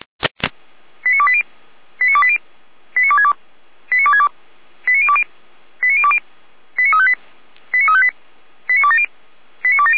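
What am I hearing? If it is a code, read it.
Five-tone selective call (ZVEI), heard as quick runs of five tones, tones of about 70 ms. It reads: 89280 89280 89262 89262 89020 89020 89358 89358 89270 89270
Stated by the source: ZVEI five-tone selective call 89330 received by SDR.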